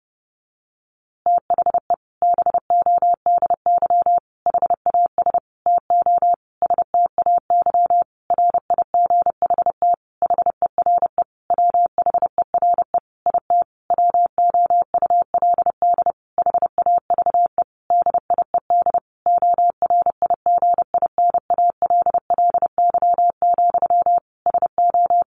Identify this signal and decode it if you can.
Morse code, heard as a keyed tone, about 30 wpm, 700 Hz. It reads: THE BODY HAS TO STAY RIGHT HERE WHERE IT WOULD HAVE DIED ORIGINALLY, SO
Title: A Sound Of Thunder 30wpm